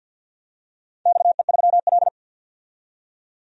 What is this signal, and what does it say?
Morse code, heard as a keyed tone, about 51 wpm, 700 Hz. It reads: XE3L
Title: DXCC-100 Call Signs - 4 Characters - 50wpm